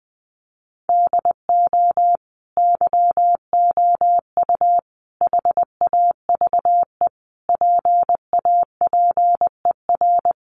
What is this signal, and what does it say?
Morse code, heard as a keyed tone, about 20 wpm, 700 Hz. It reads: DO YOU HAVE PAPER
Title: Sentences from Top 1500 Words 20wpm